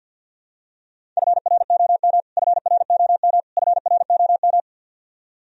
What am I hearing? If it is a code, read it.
Morse code, heard as a keyed tone, about 52 wpm, 700 Hz. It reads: FROM FROM FROM
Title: Top 100 Words Repeated 50wpm